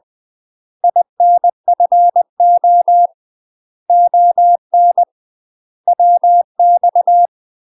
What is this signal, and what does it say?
Morse code, heard as a keyed tone, about 20 wpm, 700 Hz. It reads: INFO ON WX
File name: UK MASTER QSOs TRAINING / 20WPM / UK MASTER QSOs 0014